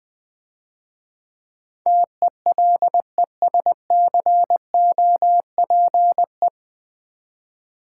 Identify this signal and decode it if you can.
Morse code, heard as a keyed tone, about 21 wpm, 700 Hz. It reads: TELESCOPE
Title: Difficult Words - Speed-Racing with Hints - 20wpm